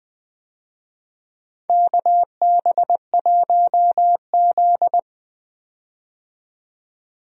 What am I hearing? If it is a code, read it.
Morse code, heard as a keyed tone, about 20 wpm, 700 Hz. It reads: KB1Z
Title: USA Call Signs - 4 Characters - 20wpm